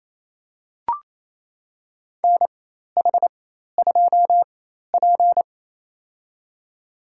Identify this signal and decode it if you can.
Morse code, heard as a keyed tone, about 28 wpm, spaced as for 13 wpm, 700 Hz. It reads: NH2P
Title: USA Call Signs - 4 Characters - 4x Character Spacing - 28wpm